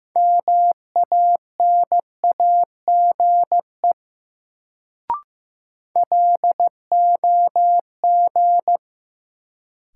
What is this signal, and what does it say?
Morse code, heard as a keyed tone, about 15 wpm, 700 Hz. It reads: MANAGE LOG
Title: Sentences from Top 1400 Words - New Word Drill Alt - 15wpm